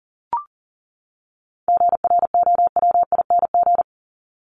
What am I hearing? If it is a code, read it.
Morse code, heard as a keyed tone, about 40 wpm, 700 Hz. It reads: GROWING
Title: ING Suffix - Top 100 Words with Base Words 40wpm